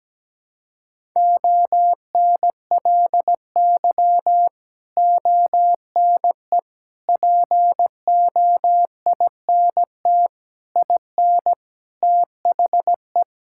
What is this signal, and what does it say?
Morse code, heard as a keyed tone, about 17 wpm, 700 Hz. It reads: ONLY ONE POINT IN THE
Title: Sentences from Top 1000 Words - Review Part 5 - 17wpm